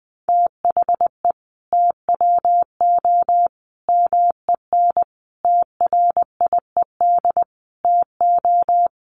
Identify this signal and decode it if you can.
Morse code, heard as a keyed tone, about 20 wpm, 700 Hz. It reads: THE TWO MEN TRIED TO
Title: A Sound Of Thunder 20wpm